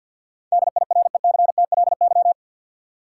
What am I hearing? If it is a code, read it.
Morse code, heard as a keyed tone, about 50 wpm, 700 Hz. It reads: DIRECTLY
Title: Sentences from Top 800 Words - New Word Drill - 50wpm